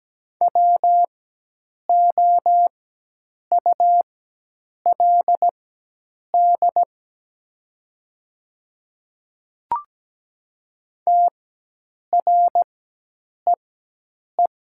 Morse code, heard as a keyed tone, about 17 wpm, spaced as for 8 wpm, 700 Hz. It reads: WOULD TREE
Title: 4x Character Spacing - Top 300 Words - 17wpm